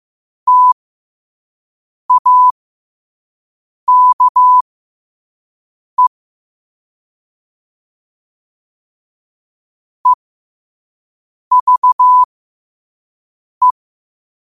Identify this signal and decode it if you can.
Morse code, heard as a keyed tone, about 15 wpm, spaced as for 5 wpm, 1000 Hz. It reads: TAKE EVE
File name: qotdfiles / 5wpm